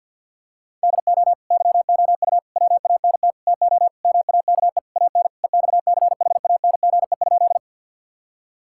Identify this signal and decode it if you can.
Morse code, heard as a keyed tone, about 50 wpm, 700 Hz. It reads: DO YOU WANT TO MAKE AN EXCHANGE?
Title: Sentences from Top 1000 Words - Review Part 6 - 50wpm